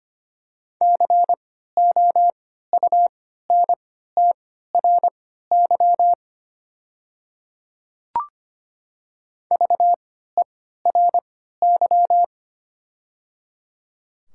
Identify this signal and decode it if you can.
Morse code, heard as a keyed tone, about 25 wpm, spaced as for 14 wpm, 690 Hz. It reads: COUNTRY VERY
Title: 3x Character Spacing - Top 300 Words - 25wpm